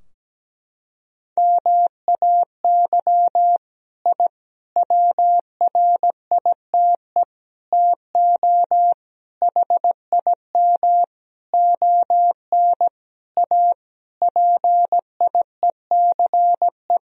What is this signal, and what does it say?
Morse code, heard as a keyed tone, about 17 wpm, 700 Hz. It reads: MAY I WRITE TO HIM ON A PIECE
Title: Sentences from Top 1000 Words - Review Part 2 - 17wpm